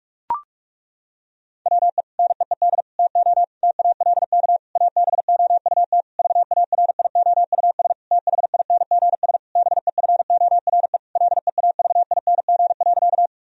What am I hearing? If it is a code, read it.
Morse code, heard as a keyed tone, about 41 wpm, 700 Hz. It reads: WE NEED TO TALK ABOUT VARIOUS THINGS BEFORE LEAVING.
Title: Sentences from Top 1000 Words - Review Part 4 - 45wpm